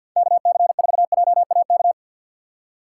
Morse code, heard as a keyed tone, about 53 wpm, 700 Hz. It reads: KC4JAK
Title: USA Call Signs - 6x Character Spacing - 50wpm